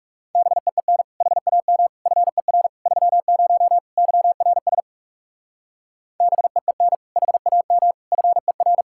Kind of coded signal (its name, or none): Morse code